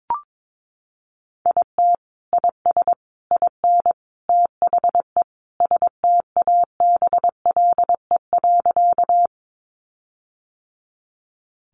Morse code, heard as a keyed tone, about 22 wpm, 700 Hz. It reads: IT IS IN THE STABLE.
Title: Sentences from Top 1500 Words 22wpm